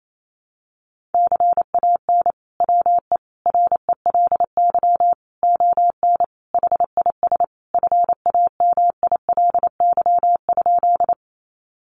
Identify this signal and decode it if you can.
Morse code, heard as a keyed tone, about 28 wpm, 700 Hz. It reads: CAN WE RELY ON HIS FAMILY?
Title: Sentences from Top 1200 Words 28wpm